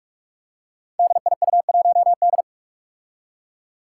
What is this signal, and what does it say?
Morse code, heard as a keyed tone, about 42 wpm, 700 Hz. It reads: DIU1D
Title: Five Challenging Letter-Number Combinations 45wpm